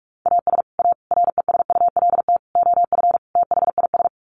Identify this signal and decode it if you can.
Morse code, heard as a keyed tone, about 44 wpm, 700 Hz. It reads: AS A RESULT OF THIS
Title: Sets of 5 Words 45wpm